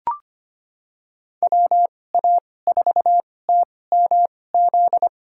Morse code, heard as a keyed tone, about 25 wpm, spaced as for 18 wpm, 700 Hz. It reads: WA4TMZ
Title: USA Call Signs - 6 Characters - 2x Character Spacing - 25wpm